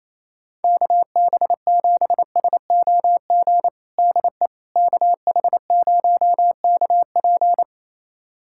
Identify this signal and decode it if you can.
Morse code, heard as a keyed tone, about 27 wpm, 700 Hz. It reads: KB7SOG DE KH0KP
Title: USA Callsign DE Callsign 28wpm